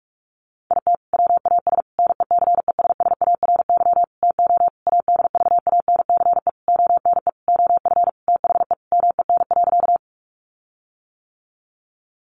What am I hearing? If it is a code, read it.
Morse code, heard as a keyed tone, about 44 wpm, 700 Hz. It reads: IT WAS NECESSARY TO ADVANCE ONE OF THE MEN.
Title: Sentences from Top 1400 Words 45wpm